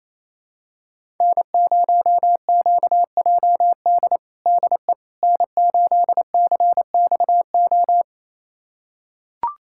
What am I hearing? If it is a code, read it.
Morse code, heard as a keyed tone, about 28 wpm, 700 Hz. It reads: N0QJD DE N8CXO
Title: USA Callsign DE Callsign - 5 Characters - 28wpm